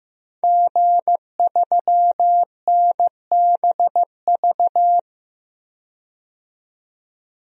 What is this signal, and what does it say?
Morse code, heard as a keyed tone, about 15 wpm, 700 Hz. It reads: G3NBV
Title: DXCC-100 Call Signs - 5 Characters - 15wpm